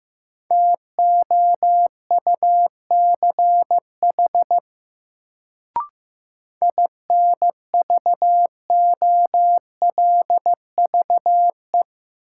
Morse code, heard as a keyed tone, about 15 wpm, 700 Hz. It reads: TOUCH INVOLVE
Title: Sentences from Top 1200 Words - New Word Drill Alt - 15wpm